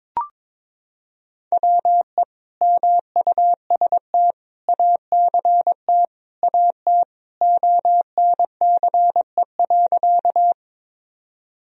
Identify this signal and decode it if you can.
Morse code, heard as a keyed tone, about 22 wpm, 700 Hz. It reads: WE MUST ACT AT ONCE.
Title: Sentences from Top 1000 Words - Review Part 1 - 22wpm